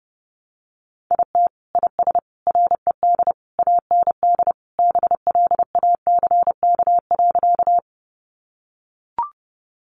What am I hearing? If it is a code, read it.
Morse code, heard as a keyed tone, about 30 wpm, 700 Hz. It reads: IT IS RED AND BLACK.
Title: Additional Easy Phrases 30wpm